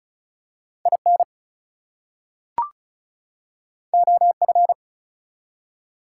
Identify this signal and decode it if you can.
Morse code, heard as a keyed tone, about 36 wpm, 700 Hz. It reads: IN OF
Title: Two Letter Words - 35wpm